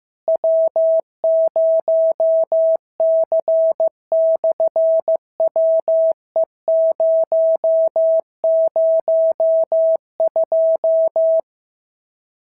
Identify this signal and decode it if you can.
Morse code, heard as a keyed tone, about 15 wpm, 650 Hz. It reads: W0C/WE002
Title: SOTA QSO Practice 15wpm